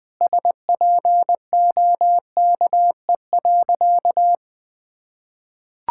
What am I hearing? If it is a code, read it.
Morse code, heard as a keyed tone, about 20 wpm, 700 Hz. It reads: SPOKE.
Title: Sentences with 3-Word Phrases - Spoken Beginning - 20wpm